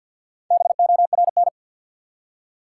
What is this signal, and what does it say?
Morse code, heard as a keyed tone, about 53 wpm, 700 Hz. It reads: BORN
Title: Four Letter Words - 50wpm